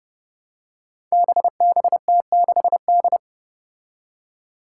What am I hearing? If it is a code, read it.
Morse code, heard as a keyed tone, about 30 wpm, 700 Hz. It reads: BBT6D
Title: Five Challenging Letter-Number Combinations 30wpm